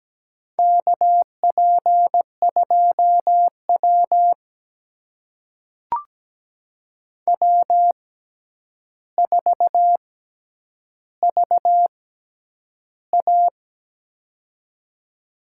Morse code, heard as a keyed tone, about 17 wpm, spaced as for 9 wpm, 700 Hz. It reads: KP2W W4VA
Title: USA Call Signs - 4 Characters - 6x Character Spacing - 17wpm